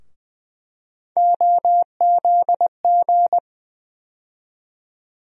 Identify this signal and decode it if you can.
Morse code, heard as a keyed tone, about 20 wpm, 700 Hz. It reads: OZG
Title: USA Call Sign Postfixes - 3 Letters 20wpm